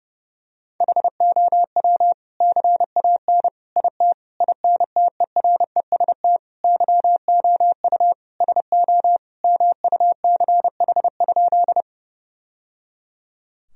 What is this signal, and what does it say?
Morse code, heard as a keyed tone, about 32 wpm, 700 Hz. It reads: HOW CAN IT INTEREST YOU SO MUCH?
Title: Sentences from Top 1000 Words - Review Part 2 - 30wpm